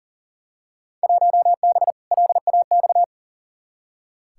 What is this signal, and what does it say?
Morse code, heard as a keyed tone, about 40 wpm, 700 Hz. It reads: 1B LAX
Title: Field Day Exchanges - 40wpm